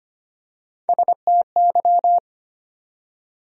Morse code, heard as a keyed tone, about 25 wpm, 700 Hz. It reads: STY
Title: USA Call Sign Postfixes - 3 Letters 25wpm